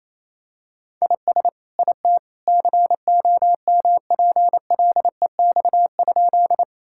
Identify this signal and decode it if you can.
Morse code, heard as a keyed tone, about 28 wpm, 700 Hz. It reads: IS IT COMPLEX?
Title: Sentences from Top 1000 Words - Review Part 6 - 28wpm